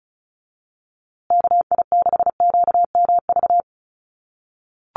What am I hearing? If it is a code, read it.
Morse code, heard as a keyed tone, about 34 wpm, 700 Hz. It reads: KI6QMV
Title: USA Call Signs Repeated 3x 35wpm